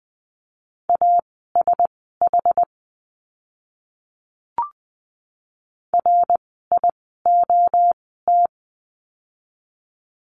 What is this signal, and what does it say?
Morse code, heard as a keyed tone, about 20 wpm, spaced as for 14 wpm, 700 Hz. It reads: ASH RIOT
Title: 061 Lesson 11f - Letter h - Words 20wpm